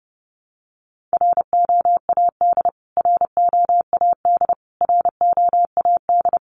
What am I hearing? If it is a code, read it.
Morse code, heard as a keyed tone, about 30 wpm, 700 Hz. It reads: ROAD ROAD ROAD
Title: Top 500 Words Repeated 30wpm